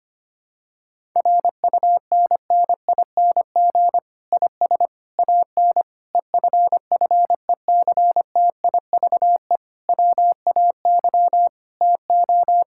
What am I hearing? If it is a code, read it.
Morse code, heard as a keyed tone, about 25 wpm, 700 Hz. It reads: RUNNING IS AN EFFECTIVE WAY TO
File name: practice / 25wpm / Difficult Sentences with Hints - 25wpm